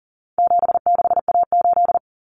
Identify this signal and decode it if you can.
Morse code, heard as a keyed tone, about 40 wpm, 700 Hz. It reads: 76A8
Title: Common Cut Numbers - 4-Digit 40wpm